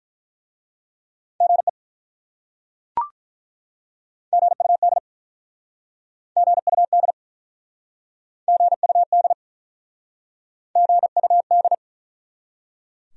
Morse code, heard as a keyed tone, about 48 wpm, 700 Hz. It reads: GE GUD GUD GUD GUD